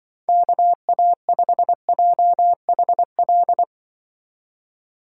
Morse code, heard as a keyed tone, about 24 wpm, 700 Hz. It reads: KA5JHL
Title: Speed Racing - USA 2x3 Call Signs 20wpm